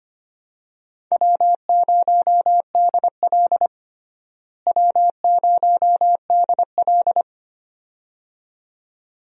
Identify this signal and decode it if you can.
Morse code, heard as a keyed tone, about 25 wpm, 700 Hz. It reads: W0DL W0DL
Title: USA Call Signs - 4 Characters - Repeated 2x - 25wpm